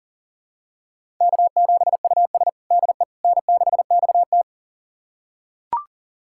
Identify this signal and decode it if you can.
Morse code, heard as a keyed tone, about 40 wpm, 700 Hz. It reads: K7US DE N6XT
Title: USA Callsign DE Callsign - 4 Characters - 40wpm